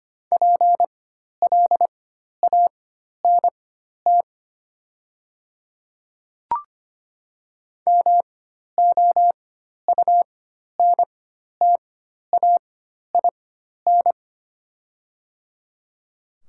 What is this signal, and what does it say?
Morse code, heard as a keyed tone, about 25 wpm, spaced as for 12 wpm, 700 Hz. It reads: PLANT MOUNTAIN
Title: 4x Character Spacing - Top 500 Words - 25wpm